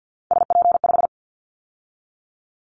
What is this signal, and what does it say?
Morse code, heard as a keyed tone, about 50 wpm, 700 Hz. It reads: SP5
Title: Callsign Prefixes for Top 25 DXCC - 50wpm